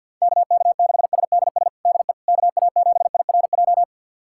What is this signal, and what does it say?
Morse code, heard as a keyed tone, about 51 wpm, 690 Hz. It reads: KK6SDS DE KA7IRJ
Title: USA Callsign DE Callsign - 6 Characters - 50wpm